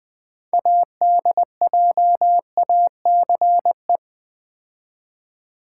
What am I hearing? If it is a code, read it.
Morse code, heard as a keyed tone, about 20 wpm, 700 Hz. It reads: ADJACE
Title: Build A Word - 3 Letters At A Time - 8-Letter Words - 20wpm